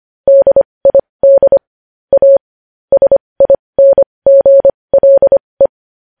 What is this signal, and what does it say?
Morse code, heard as a keyed tone, about 25 wpm, spaced as for 20 wpm, 550 Hz. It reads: DID A SINGLE
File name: podcast / News headlines-1x-25x20